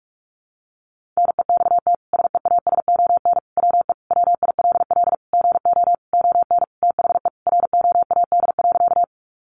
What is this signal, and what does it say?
Morse code, heard as a keyed tone, about 44 wpm, 700 Hz. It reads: NEXT SEASON WE WILL GO ON THE ROAD.
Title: Sentences from Top 1400 Words 45wpm